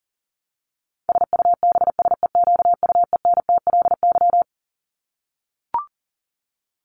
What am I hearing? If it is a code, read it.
Morse code, heard as a keyed tone, about 40 wpm, 700 Hz. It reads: SUBSEQUENTLY
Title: LY Suffix - Top 100 Words with Base Words 40wpm